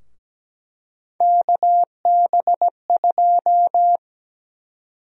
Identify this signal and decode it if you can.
Morse code, heard as a keyed tone, about 17 wpm, 700 Hz. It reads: KB2
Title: USA Call Sign Prefixes - Repeated 2x - 17wpm